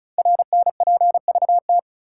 Morse code, heard as a keyed tone, about 36 wpm, 700 Hz. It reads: RNPVT
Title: Code Groups - 5 Letters - 35wpm